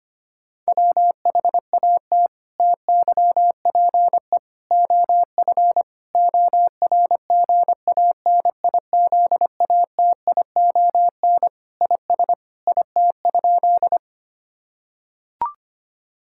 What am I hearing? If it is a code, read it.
Morse code, heard as a keyed tone, about 25 wpm, 700 Hz. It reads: WHAT TYPE OF ORGANIZATION IS IT?
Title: Sentences from Top 1000 Words - Review Part 5 - 25wpm